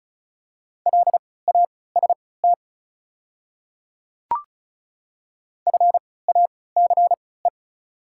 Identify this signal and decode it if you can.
Morse code, heard as a keyed tone, about 36 wpm, spaced as for 20 wpm, 690 Hz. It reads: LAST FACE
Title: 3x Character Spacing - Top 300 Words - 35wpm